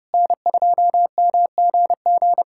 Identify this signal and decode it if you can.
Morse code, heard as a keyed tone, about 31 wpm, 700 Hz. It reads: N2MGG
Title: USA Callsign DE Callsign 30wpm